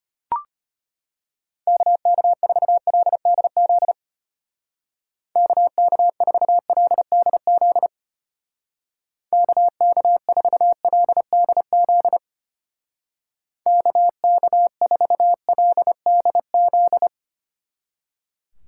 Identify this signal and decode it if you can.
Morse code, heard as a keyed tone, about 31 wpm, 700 Hz. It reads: KK4LDZ KK4LDZ KK4LDZ KK4LDZ
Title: Speed Racing - USA 2x3 Call Signs 25wpm